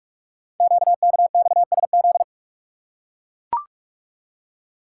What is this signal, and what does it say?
Morse code, heard as a keyed tone, about 44 wpm, 700 Hz. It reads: QKXSZ
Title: Code Groups - 5 Letters - 45wpm